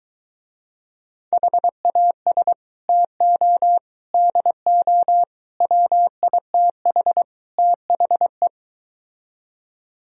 Morse code, heard as a keyed tone, about 23 wpm, 700 Hz. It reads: HAS TO DO WITH THE